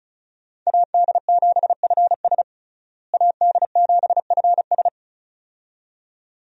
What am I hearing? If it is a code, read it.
Morse code, heard as a keyed tone, about 34 wpm, 700 Hz. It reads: AD7FS AD7FS
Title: USA Call Signs - 5 Characters - Repeated 2x - 35wpm